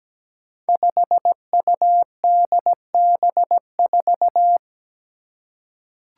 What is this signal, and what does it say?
Morse code, heard as a keyed tone, about 17 wpm, 700 Hz. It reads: 5UDB4
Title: Five Challenging Letter-Number Combinations 17wpm